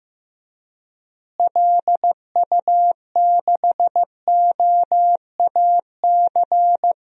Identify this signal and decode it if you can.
Morse code, heard as a keyed tone, about 15 wpm, 700 Hz. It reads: LU6OAC
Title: DXCC-100 Call Signs - 6 Characters - Repeated 2x - 15wpm